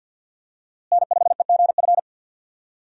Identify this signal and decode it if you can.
Morse code, heard as a keyed tone, about 51 wpm, 700 Hz. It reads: N5EGF
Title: USA Call Signs - 5 Characters - Repeated 2x - 50wpm